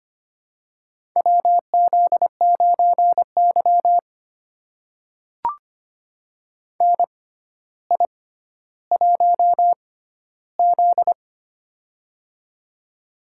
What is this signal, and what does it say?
Morse code, heard as a keyed tone, about 25 wpm, spaced as for 13 wpm, 700 Hz. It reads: WZ9Y NI1Z
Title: USA Call Signs - 4 Characters - 6x Character Spacing - 25wpm